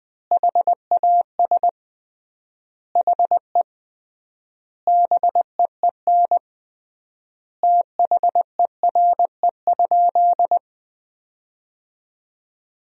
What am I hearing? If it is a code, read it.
Morse code, heard as a keyed tone, about 20 wpm, 700 Hz. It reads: HAS HE BEEN THERE?
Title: 3x Word-Spacing - Sentences from Top 100 Words - 20wpm